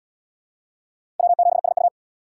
Morse code, heard as a keyed tone, about 66 wpm, 700 Hz. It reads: W7SU